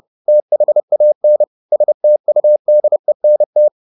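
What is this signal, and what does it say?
Morse code, heard as a keyed tone, about 30 wpm, 600 Hz. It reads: THAN STUDENT
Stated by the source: Morse code at 30 wpm, the novel Never by Ken Follett in English, from CW med Gnister.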